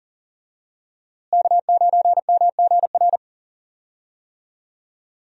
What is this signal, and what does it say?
Morse code, heard as a keyed tone, about 40 wpm, 700 Hz. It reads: K9MGR
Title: USA Call Signs - 5 Characters - Repeated 3x - 40wpm